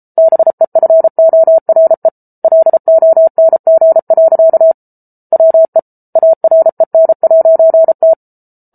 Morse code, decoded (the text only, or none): BEFORE LONG. WE AREN'T